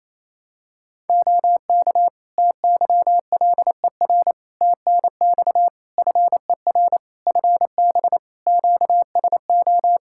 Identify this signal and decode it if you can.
Morse code, heard as a keyed tone, about 28 wpm, 700 Hz. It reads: OK TYLER TNX FER FB QSO
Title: Practice QSO - Exchange 4 - Ending - 28wpm